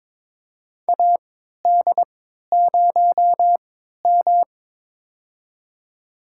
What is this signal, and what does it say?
Morse code, heard as a keyed tone, about 23 wpm, spaced as for 13 wpm, 700 Hz. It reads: AD0M
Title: USA Call Signs - 4 Characters - 3x Character Spacing - 22wpm